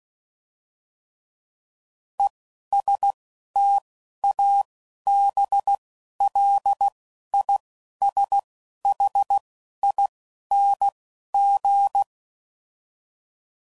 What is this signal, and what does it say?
Morse code, heard as a keyed tone, about 16 wpm, 800 Hz. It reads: E S T A B L I S H I N G